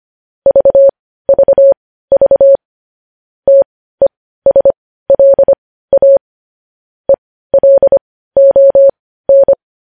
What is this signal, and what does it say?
Morse code, heard as a keyed tone, about 25 wpm, spaced as for 15 wpm, 550 Hz. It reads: VVV TESLA ELON
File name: podcast / News headlines-1x-25x15